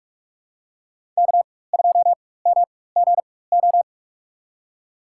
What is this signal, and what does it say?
Morse code, heard as a keyed tone, about 44 wpm, spaced as for 21 wpm, 700 Hz. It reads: K2MGO
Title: USA Call Signs - 5 Characters - 4x Character Spacing - 45wpm